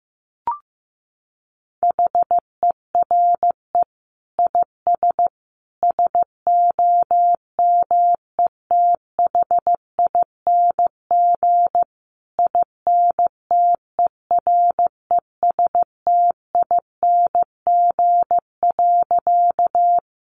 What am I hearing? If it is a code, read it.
Morse code, heard as a keyed tone, about 15 wpm, 700 Hz. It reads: HERE IS SOMETHING INTERESTING.
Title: Sentences from Top 1000 Words 15wpm